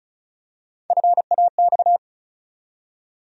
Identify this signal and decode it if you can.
Morse code, heard as a keyed tone, about 36 wpm, 700 Hz. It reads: FAX
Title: USA Call Sign Postfixes - 3 Letters 35wpm